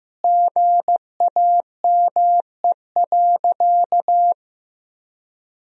Morse code, heard as a keyed tone, about 15 wpm, 700 Hz. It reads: GAME.
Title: 3x Word-Spacing - Sentences from Top 500 Words - 15wpm